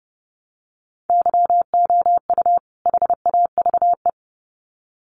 Morse code, heard as a keyed tone, about 30 wpm, 700 Hz. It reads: YOU HAVE
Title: Sets of 2 Words 30wpm